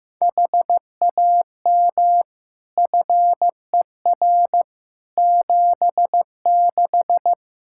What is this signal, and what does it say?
Morse code, heard as a keyed tone, about 15 wpm, 700 Hz. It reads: HAM FER 76
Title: Practice QSO - Exchange 3 - More Info - 15wpm